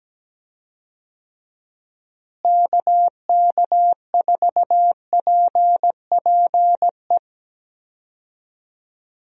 Morse code, heard as a keyed tone, about 17 wpm, 700 Hz. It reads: KK4PPE
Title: USA Call Signs - 6 Characters - Repeated 3x - 17wpm